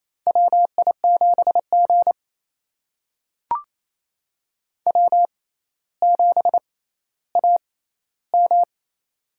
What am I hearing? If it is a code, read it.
Morse code, heard as a keyed tone, about 28 wpm, spaced as for 14 wpm, 700 Hz. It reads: WI7G W7AM
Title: USA Call Signs - 4 Characters - 6x Character Spacing - 28wpm